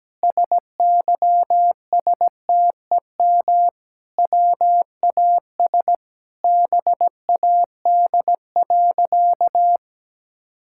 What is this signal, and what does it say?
Morse code, heard as a keyed tone, about 17 wpm, 700 Hz. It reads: SYSTEM WAS BAD.
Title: Sentences from Top 1000 Words - Review Part 5 - 17wpm